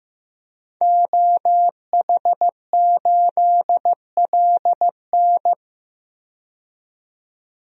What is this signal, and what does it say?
Morse code, heard as a keyed tone, about 15 wpm, 700 Hz. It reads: OH8LN
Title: DXCC-100 Call Signs - 5 Characters - 15wpm